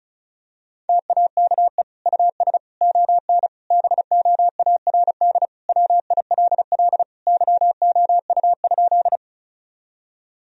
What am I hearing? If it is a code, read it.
Morse code, heard as a keyed tone, about 34 wpm, 700 Hz. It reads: TAKE US ON BOARD WILL YOU?
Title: Sentences from Top 1000 Words - Review Part 4 - 35wpm